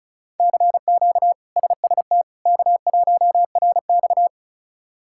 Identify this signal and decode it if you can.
Morse code, heard as a keyed tone, about 36 wpm, 700 Hz. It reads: CQ SST K1RX
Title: K1USN SST QSO Practice - 35wpm